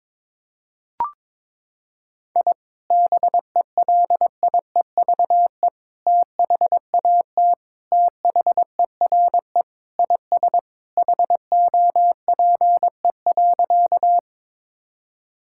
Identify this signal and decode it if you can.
Morse code, heard as a keyed tone, about 22 wpm, 700 Hz. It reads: I BELIEVE THAT THERE IS HOPE.
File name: practice / 22wpm / Sentences with 3-Word Phrases - 22wpm